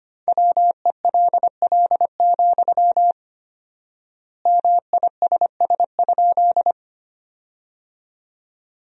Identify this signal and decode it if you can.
Morse code, heard as a keyed tone, about 25 wpm, 700 Hz. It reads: WELL, MISS?
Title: 4x Word-Spacing - Sentences from Top 500 Words - 25wpm